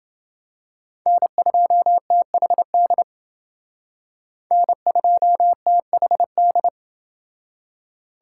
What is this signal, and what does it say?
Morse code, heard as a keyed tone, about 29 wpm, 700 Hz. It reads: N2THD N2THD